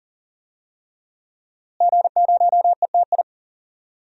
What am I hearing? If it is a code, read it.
Morse code, heard as a keyed tone, about 37 wpm, 700 Hz. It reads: G0ETI
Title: DXCC-100 Call Signs - 5 Characters - Repeated 3x - 40wpm